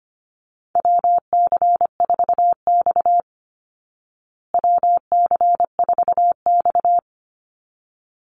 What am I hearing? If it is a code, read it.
Morse code, heard as a keyed tone, about 25 wpm, 700 Hz. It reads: WC4X WC4X
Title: Call Signs Repeated 3x - 25wpm